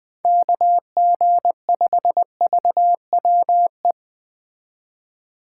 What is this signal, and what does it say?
Morse code, heard as a keyed tone, about 21 wpm, 700 Hz. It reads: KG5VWE
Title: Speed Racing - USA Call Signs 17wpm